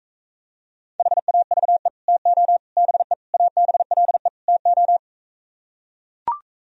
Morse code, heard as a keyed tone, about 45 wpm, 700 Hz. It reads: HAVE TO BE ABLE TO